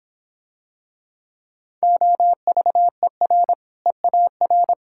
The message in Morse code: OVER EAR

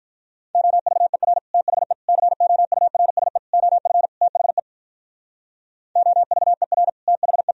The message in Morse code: OVER THE COURSE OF THE OVER THE

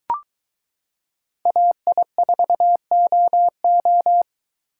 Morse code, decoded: AI4OO

